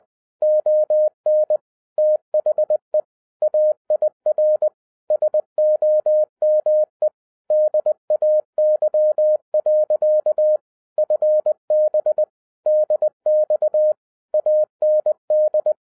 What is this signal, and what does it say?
Morse code, heard as a keyed tone, about 20 wpm, 610 Hz. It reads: ON THE AIR SOME DAY. FB DX AND
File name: 2012-01-mp3 / n05-CAME BACK TO HAM RADIO AFTER 30 YEARS OF INTERVAL-20wpm